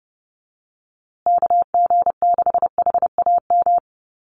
Morse code, heard as a keyed tone, about 30 wpm, 700 Hz. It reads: KG6HAM
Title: USA Call Signs Repeated 2x 30wpm